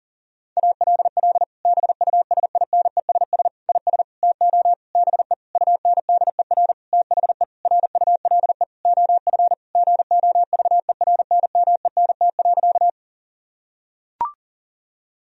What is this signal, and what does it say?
Morse code, heard as a keyed tone, about 40 wpm, 700 Hz. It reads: ALL BUSINESS IS TO BE UNDER THE RULE OF GOVERNMENT.
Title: Sentences from Top 1000 Words - Review Part 3 - 40wpm